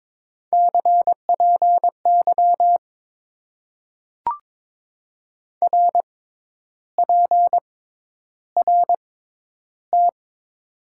Morse code, heard as a keyed tone, about 21 wpm, spaced as for 10 wpm, 700 Hz. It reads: CPY RPRT
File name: Top 90 QSO Elements - 6x Character Spacing - 22wpm